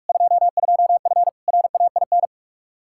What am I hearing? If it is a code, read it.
Morse code, heard as a keyed tone, about 41 wpm, 700 Hz. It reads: 22F RAIN
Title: Practice QSO - Exchange 2 - Info - 45wpm